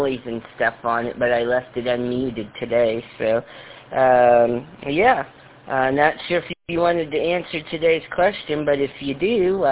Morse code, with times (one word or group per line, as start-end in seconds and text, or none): none